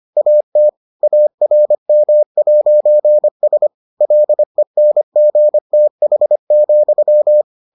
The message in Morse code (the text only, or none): AT ARM'S LENGTH,